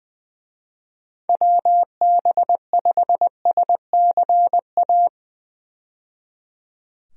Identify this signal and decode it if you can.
Morse code, heard as a keyed tone, about 20 wpm, 700 Hz. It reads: WB5SCA
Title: USA Call Signs - 6 Characters - 20wpm